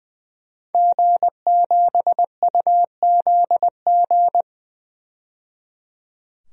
Morse code, heard as a keyed tone, about 20 wpm, 700 Hz. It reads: G7UZG
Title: DXCC-100 Call Signs - 5 Characters - 20wpm